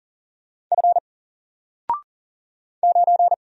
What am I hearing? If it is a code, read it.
Morse code, heard as a keyed tone, about 40 wpm, 700 Hz. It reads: F 9
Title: Single Letter-Number Alt 40wpm